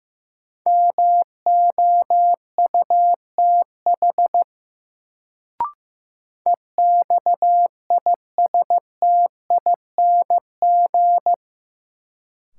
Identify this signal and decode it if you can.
Morse code, heard as a keyed tone, about 15 wpm, 700 Hz. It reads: MOUTH EXISTING
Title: Sentences from Top 1500 Words - New Word Drill Alt - 15wpm